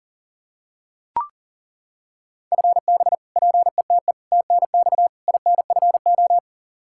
Morse code, heard as a keyed tone, about 40 wpm, 700 Hz. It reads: FB PETE TNX INFO
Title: Practice QSO - Exchange 3 - More Info - 40wpm